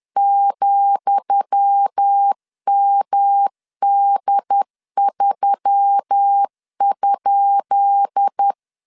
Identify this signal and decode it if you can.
Morse code, heard as a keyed tone, about 10 wpm, 800 Hz. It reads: ,MD3?
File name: LEZIONI / 10 SIGNS / 10D COMA 10wpm 3013